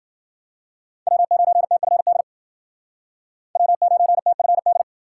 Morse code, heard as a keyed tone, about 51 wpm, 700 Hz. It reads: W9TFD W9TFD